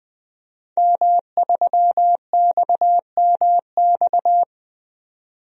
Morse code, heard as a keyed tone, about 20 wpm, 700 Hz. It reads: M3XMX
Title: DXCC-100 Call Signs - 5 Characters - 20wpm